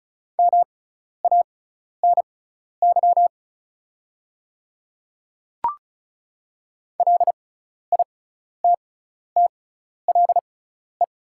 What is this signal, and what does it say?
Morse code, heard as a keyed tone, about 36 wpm, spaced as for 12 wpm, 690 Hz. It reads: MANY LITTLE
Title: 6x Character Spacing - Top 300 Words - 35wpm